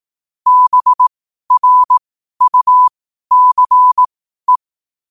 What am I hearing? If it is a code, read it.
Morse code, heard as a keyed tone, about 18 wpm, spaced as for 12 wpm, 1000 Hz. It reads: BRUCE
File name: 12wpm